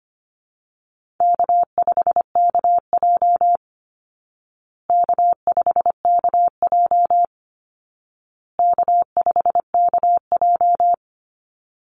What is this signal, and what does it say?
Morse code, heard as a keyed tone, about 25 wpm, 700 Hz. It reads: K5KJ K5KJ K5KJ
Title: Call Signs Repeated 2x - 25wpm